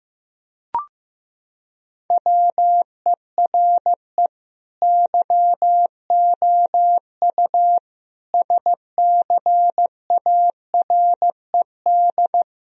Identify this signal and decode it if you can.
Morse code, heard as a keyed tone, about 15 wpm, 700 Hz. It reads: WERE YOU SCARED
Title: Sentences from Top 1000 Words - Review Part 4 - 15wpm